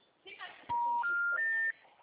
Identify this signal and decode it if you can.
Special information tone, with speech.